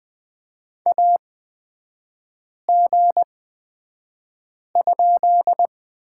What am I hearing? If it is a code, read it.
Morse code, heard as a keyed tone, about 20 wpm, spaced as for 5 wpm, 700 Hz. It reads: AG?
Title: 243 Lesson 39e - Letter z - Three Characters 20wpm